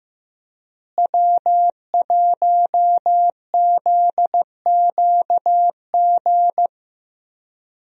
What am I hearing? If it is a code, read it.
Morse code, heard as a keyed tone, about 16 wpm, 700 Hz. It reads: W1ZQG
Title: USA Call Signs - 5 Characters - Repeated 3x - 15wpm